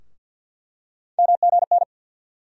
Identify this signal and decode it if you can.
Morse code, heard as a keyed tone, about 47 wpm, 700 Hz. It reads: MGN
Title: USA Call Sign Postfixes - 3 Letters 50wpm